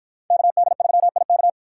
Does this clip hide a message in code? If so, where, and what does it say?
Yes: Morse code from 0.3 to 1.5 s, heard as a keyed tone, about 50 wpm, 700 Hz. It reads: KD3IK